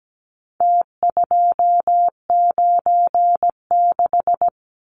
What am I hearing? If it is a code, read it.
Morse code, heard as a keyed tone, about 17 wpm, 700 Hz. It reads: T296
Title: Numbers - Contest Style - 4-Digit 17wpm